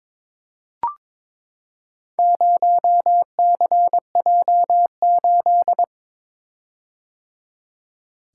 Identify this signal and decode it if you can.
Morse code, heard as a keyed tone, about 22 wpm, 700 Hz. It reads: 0CJ8
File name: practice / 22wpm / Four Letter-Number Combinations - 22wpm